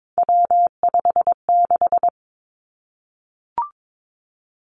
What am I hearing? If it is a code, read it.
Morse code, heard as a keyed tone, about 22 wpm, 700 Hz. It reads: W56
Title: Three Challenging Letter-Number Combinations 22wpm